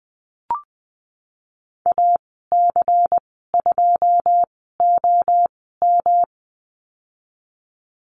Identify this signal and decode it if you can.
Morse code, heard as a keyed tone, about 20 wpm, spaced as for 14 wpm, 700 Hz. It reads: AC2OM
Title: 161 Lesson 26g - Letter g - Call Signs 20wpm